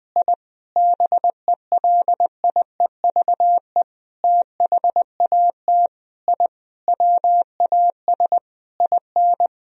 Morse code, heard as a keyed tone, about 20 wpm, 700 Hz. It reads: I BELIEVE THAT I WAS IN
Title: Sentences from Top 1000 Words - Review Part 1 - 20wpm